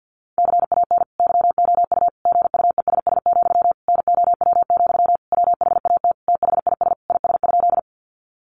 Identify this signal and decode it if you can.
Morse code, heard as a keyed tone, about 50 wpm, 700 Hz. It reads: CAN YOU GUESS, NOW, WHAT THIS IS?
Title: Sentences from Top 1500 Words 50wpm